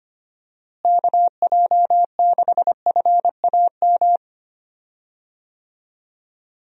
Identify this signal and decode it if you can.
Morse code, heard as a keyed tone, about 25 wpm, 700 Hz. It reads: KJ6FAM
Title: USA Call Signs - 6 Characters - Repeated 2x - 25wpm